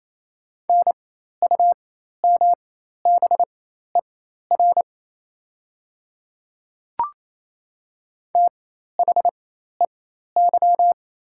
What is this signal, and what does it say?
Morse code, heard as a keyed tone, about 27 wpm, spaced as for 13 wpm, 700 Hz. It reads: NUMBER THEY
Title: 4x Character Spacing - Top 100 Words - 28wpm